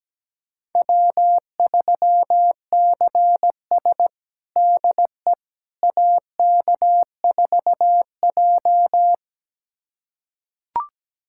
Morse code, heard as a keyed tone, about 17 wpm, 700 Hz. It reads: W3CS DE AK4J
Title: USA Callsign DE Callsign - 4 Characters - 17wpm